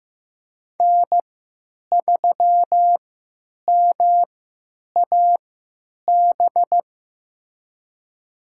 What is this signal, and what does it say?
Morse code, heard as a keyed tone, about 16 wpm, spaced as for 9 wpm, 700 Hz. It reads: N3MAB
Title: USA Call Signs - 5 Characters - 3x Character Spacing - 15wpm